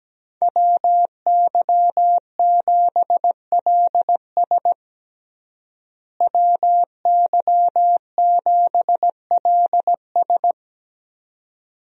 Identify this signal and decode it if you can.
Morse code, heard as a keyed tone, about 17 wpm, 700 Hz. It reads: WY7LS WY7LS
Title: USA Call Signs - 5 Characters - Repeated 2x - 17wpm